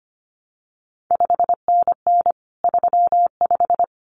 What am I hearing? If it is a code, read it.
Morse code, heard as a keyed tone, about 25 wpm, 700 Hz. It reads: 5NN 35
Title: CQ WW DX Contest Exchange - 25wpm